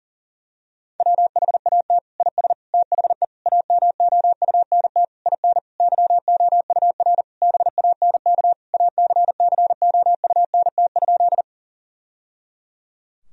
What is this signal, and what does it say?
Morse code, heard as a keyed tone, about 40 wpm, 700 Hz. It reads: WHAT IS THE AMOUNT IN YOUR BANK ACCOUNT?
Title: Sentences from Top 1000 Words - Review Part 4 - 40wpm